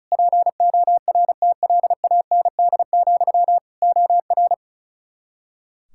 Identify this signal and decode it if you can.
Morse code, heard as a keyed tone, about 36 wpm, 700 Hz. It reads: PORTLAND, OR
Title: K1USN SST QSO Practice 35wpm